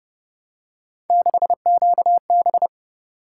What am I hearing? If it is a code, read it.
Morse code, heard as a keyed tone, about 28 wpm, 700 Hz. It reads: 6QB